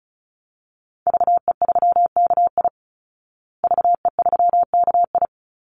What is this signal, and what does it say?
Morse code, heard as a keyed tone, about 34 wpm, 700 Hz. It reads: VE3KI VE3KI
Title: Call Signs Repeated 2x - 35wpm